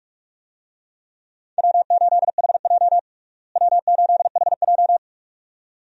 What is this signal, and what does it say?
Morse code, heard as a keyed tone, about 42 wpm, 700 Hz. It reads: W8HJ W8HJ
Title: USA Call Signs - 4 Characters - Repeated 2x - 45wpm